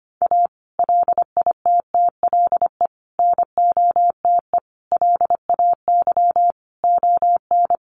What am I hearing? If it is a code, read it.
Morse code, heard as a keyed tone, about 25 wpm, 700 Hz. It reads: A LITTLE NOTE LAY ON
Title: Sentences from Top 1500 Words 25wpm